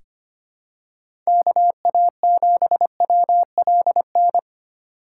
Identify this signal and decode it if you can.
Morse code, heard as a keyed tone, about 25 wpm, 700 Hz. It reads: KA7WLN
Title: USA Call Signs - 6 Characters - 3x Character Spacing - 25wpm